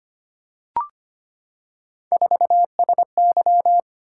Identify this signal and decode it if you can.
Morse code, heard as a keyed tone, about 25 wpm, 700 Hz. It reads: 4SY
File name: Three Letter-Number Combinations - 25wpm